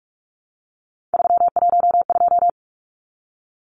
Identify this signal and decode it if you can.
Morse code, heard as a keyed tone, about 44 wpm, 700 Hz. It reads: 312
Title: Numbers - 3-Digit 45wpm